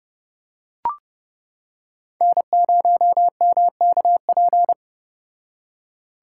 Morse code, heard as a keyed tone, about 29 wpm, 700 Hz. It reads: N0MKP